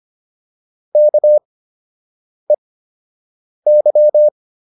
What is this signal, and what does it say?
Morse code, heard as a keyed tone, about 25 wpm, spaced as for 7 wpm, 600 Hz. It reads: KEY